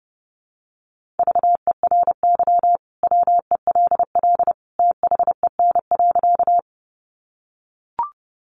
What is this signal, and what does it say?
Morse code, heard as a keyed tone, about 30 wpm, 700 Hz. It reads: VERY WELL THEN.
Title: Sentences from Top 200 Words 30wpm